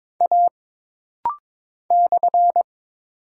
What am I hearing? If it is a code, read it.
Morse code, heard as a keyed tone, about 22 wpm, 700 Hz. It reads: A /
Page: This rapid-fire timing is geared to push your ICR (Instant Character Recognition) skills.